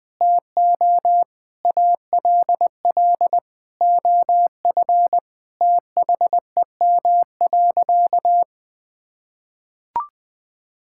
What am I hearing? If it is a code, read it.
Morse code, heard as a keyed tone, about 19 wpm, 700 Hz. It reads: TO ALL OF THEM.
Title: Sentences from Top 1000 Words - Review Part 5 - 20wpm